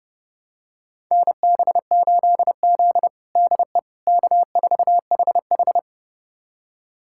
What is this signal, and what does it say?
Morse code, heard as a keyed tone, about 30 wpm, 700 Hz. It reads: NB8Z DE K4HH
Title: USA Callsign DE Callsign - 4 Characters - 30wpm